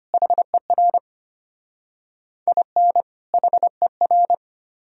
Morse code, heard as a keyed tone, about 31 wpm, 700 Hz. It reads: HER IN HER